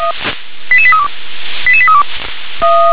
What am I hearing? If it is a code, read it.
Five-tone selective call (ZVEI), heard as quick runs of five tones, tones of about 70 ms. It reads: 88832 88832
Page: SDR/ZVEI